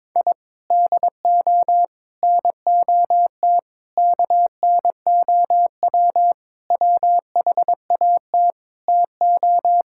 Morse code, heard as a keyed tone, about 22 wpm, 700 Hz. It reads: I DO NOT KNOW WHAT TO
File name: practice / 22wpm / Sentences with 4-Word Phrases - 22wpm